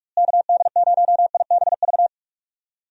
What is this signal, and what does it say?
Morse code, heard as a keyed tone, about 46 wpm, 700 Hz. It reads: KD0IBV
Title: USA Call Signs - 6 Characters - Repeated 2x - 45wpm